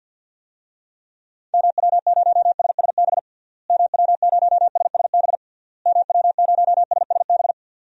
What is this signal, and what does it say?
Morse code, heard as a keyed tone, about 53 wpm, 700 Hz. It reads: MW0SSB MW0SSB MW0SSB
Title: DXCC-100 Call Signs - 6 Characters - Repeated 3x - 50wpm